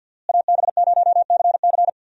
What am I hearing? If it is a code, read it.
Morse code, heard as a keyed tone, about 47 wpm, 700 Hz. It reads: AB0CC
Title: USA Call Signs - 5 Characters - Repeated 2x - 50wpm